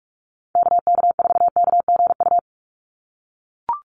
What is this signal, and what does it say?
Morse code, heard as a keyed tone, about 44 wpm, 700 Hz. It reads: KK4KGU
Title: USA Call Signs 45wpm